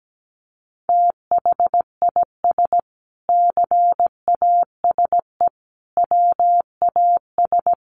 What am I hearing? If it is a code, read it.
Morse code, heard as a keyed tone, about 17 wpm, 700 Hz. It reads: THIS CASE WAS
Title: Sentences from Top 1100 Words 17wpm